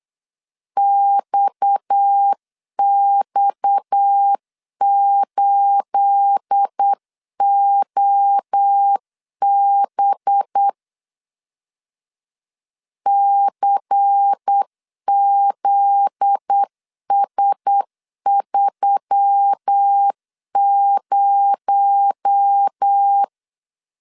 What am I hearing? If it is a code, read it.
Morse code, heard as a keyed tone, about 8 wpm, 800 Hz. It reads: XX8OB CZS30
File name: CW / DET / 1 9 1500C S A 8wpm 5637